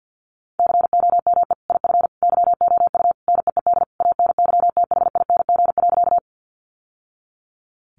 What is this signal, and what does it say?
Morse code, heard as a keyed tone, about 50 wpm, 700 Hz. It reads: COME IF YOU NEED ANYTHING.
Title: Sentences from Top 1000 Words 50wpm